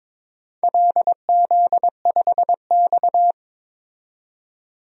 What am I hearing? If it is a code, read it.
Morse code, heard as a keyed tone, about 22 wpm, 700 Hz. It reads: LZ5X
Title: DXCC-100 Call Signs - 4 Characters - 22wpm